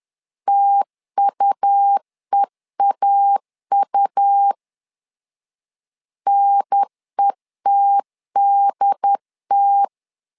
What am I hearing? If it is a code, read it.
Morse code, heard as a keyed tone, about 11 wpm, 800 Hz. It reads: TUEAU NETDT
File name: LEZIONI / 3 MWRO / 3 M 10wpm 1828